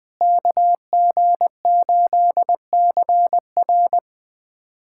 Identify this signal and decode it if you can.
Morse code, heard as a keyed tone, about 20 wpm, 700 Hz. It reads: KG8CR
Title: USA Callsign DE Callsign - 5 Characters - 20wpm